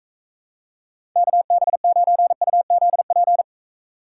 Morse code, heard as a keyed tone, about 43 wpm, 700 Hz. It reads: KB9UZP